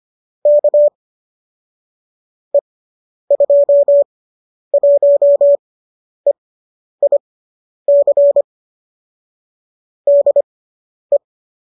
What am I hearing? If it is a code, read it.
Morse code, heard as a keyed tone, about 25 wpm, spaced as for 10 wpm, 590 Hz. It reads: K E21EIC DE
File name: qso005 10